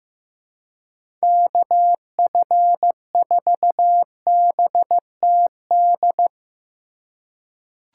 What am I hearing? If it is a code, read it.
Morse code, heard as a keyed tone, about 15 wpm, 700 Hz. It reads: KF4BTD
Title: USA Call Signs - 6 Characters - 2x Character Spacing - 15wpm